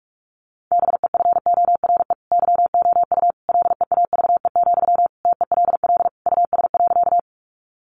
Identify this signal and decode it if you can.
Morse code, heard as a keyed tone, about 44 wpm, 700 Hz. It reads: BEFORE YOU LEAVE, TELL US.
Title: Sentences from Top 400 Words 45wpm